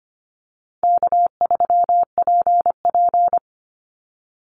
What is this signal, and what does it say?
Morse code, heard as a keyed tone, about 25 wpm, 700 Hz. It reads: K3PP
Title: Easy Call Signs - 25wpm